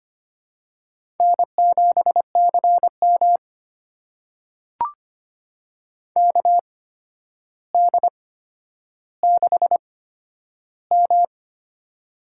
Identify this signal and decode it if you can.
Morse code, heard as a keyed tone, about 25 wpm, spaced as for 11 wpm, 700 Hz. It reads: N7CM KD6M
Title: USA Call Signs - 4 Characters - 8x Character Spacing - 25wpm